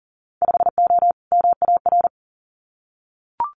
Morse code, heard as a keyed tone, about 38 wpm, 700 Hz. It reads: 5O MAR
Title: Winter Field Day Exchanges Encore 40wpm